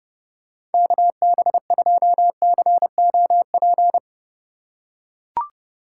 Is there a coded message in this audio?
Yes: Morse code from 0.7 to 4.0 s, heard as a keyed tone, about 29 wpm, 700 Hz. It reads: KB2COP